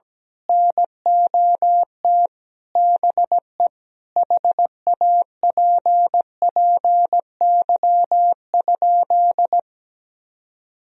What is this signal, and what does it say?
Morse code, heard as a keyed tone, about 17 wpm, 700 Hz. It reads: NOT BE HAPPY?
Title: Sentences from Top 1000 Words - Review Part 2 - 17wpm